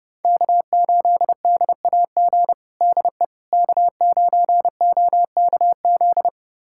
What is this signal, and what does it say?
Morse code, heard as a keyed tone, about 30 wpm, 700 Hz. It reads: K8DAG DE K9OKZ
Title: USA Callsign DE Callsign - 5 Characters - 30wpm